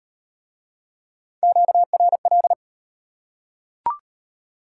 Morse code, heard as a keyed tone, about 35 wpm, 700 Hz. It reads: QRL